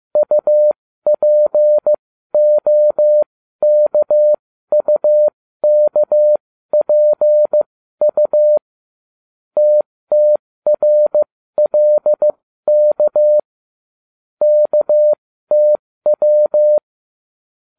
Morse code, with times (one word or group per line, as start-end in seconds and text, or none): 0.2-8.6 UPOKUKPU
9.6-13.4 TTRLK
14.4-16.8 KTW